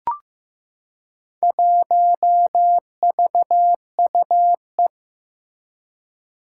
Morse code, heard as a keyed tone, about 15 wpm, 700 Hz. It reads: 1VUE
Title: Four Challenging Letter-Number Combinations 15wpm